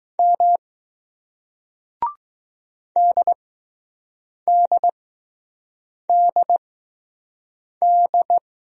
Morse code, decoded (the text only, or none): M DDDD